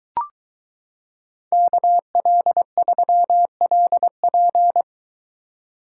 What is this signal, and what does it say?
Morse code, heard as a keyed tone, about 23 wpm, 700 Hz. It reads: KL3LP